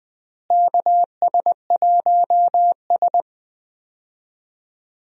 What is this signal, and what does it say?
Morse code, heard as a keyed tone, about 19 wpm, 700 Hz. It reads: KS1S
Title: USA Call Signs - 4 Characters - Repeated 2x - 20wpm